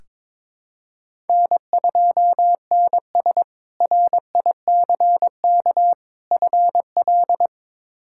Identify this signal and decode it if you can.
Morse code, heard as a keyed tone, about 22 wpm, 700 Hz. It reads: N2NS RICK FL
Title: K1USN SST QSO Practice - 22wpm